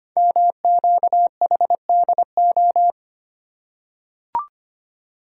Morse code, heard as a keyed tone, about 25 wpm, 700 Hz. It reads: MQHDO
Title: Code Groups - 5 Letters - 25wpm